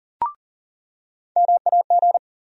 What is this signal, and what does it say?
Morse code, heard as a keyed tone, about 40 wpm, 700 Hz. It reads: MAG
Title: USA Call Sign Postfixes - 3 Letters 40wpm